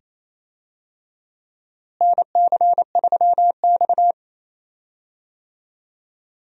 Morse code, heard as a keyed tone, about 27 wpm, 700 Hz. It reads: NC3X
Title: USA Call Signs - 4 Characters - Repeated 2x - 28wpm